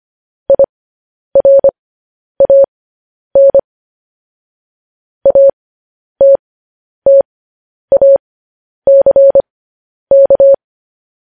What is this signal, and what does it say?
Morse code, heard as a keyed tone, about 25 wpm, spaced as for 10 wpm, 550 Hz. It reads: IRAN ATTACK
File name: podcast / News headlines-1x-25x10